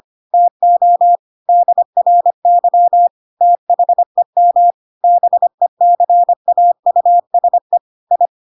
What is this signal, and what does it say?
Morse code, heard as a keyed tone, about 25 wpm, 700 Hz. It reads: TO DRY THEM BECAUSE I
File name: Pinocchio / Pinocchio ch7 25 25 ik1hge